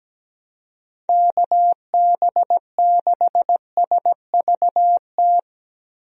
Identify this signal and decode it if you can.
Morse code, heard as a keyed tone, about 17 wpm, 700 Hz. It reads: KB6SVT
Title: USA Call Signs - 6 Characters - 17wpm